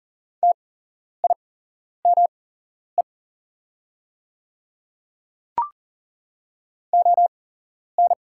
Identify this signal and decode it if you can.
Morse code, heard as a keyed tone, about 40 wpm, spaced as for 11 wpm, 700 Hz. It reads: TIME ON
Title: 8x Character Spacing - Top 100 Words - 40wpm